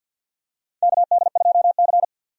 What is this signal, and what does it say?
Morse code, heard as a keyed tone, about 50 wpm, 700 Hz. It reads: KD2C
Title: USA Call Signs - 4 Characters - 50wpm